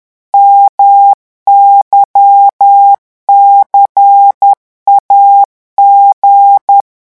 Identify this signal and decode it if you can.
Morse code, heard as a keyed tone, about 10 wpm, 800 Hz. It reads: MYCAG
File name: morse mp3 / revision3-vitesse 10 mots